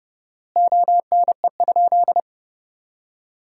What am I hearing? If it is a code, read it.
Morse code, heard as a keyed tone, about 32 wpm, 700 Hz. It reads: ONE?
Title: 4x Word-Spacing - Sentences from Top 100 Words - 30wpm